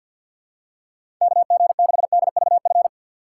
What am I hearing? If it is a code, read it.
Morse code, heard as a keyed tone, about 46 wpm, 700 Hz. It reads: KG6DVF
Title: USA Call Signs - 3x Character Spacing - 50wpm